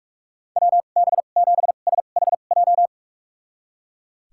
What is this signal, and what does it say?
Morse code, heard as a keyed tone, about 41 wpm, spaced as for 31 wpm, 700 Hz. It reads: WB7SHJ
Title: USA Call Signs - 6 Characters - 2x Character Spacing - 45wpm